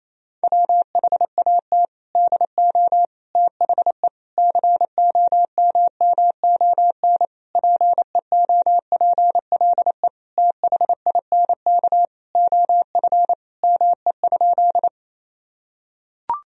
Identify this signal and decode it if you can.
Morse code, heard as a keyed tone, about 28 wpm, 700 Hz. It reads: WHAT DO THE COMMON PEOPLE THINK OF ME?
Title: Sentences from Top 1000 Words - Review Part 3 - 28wpm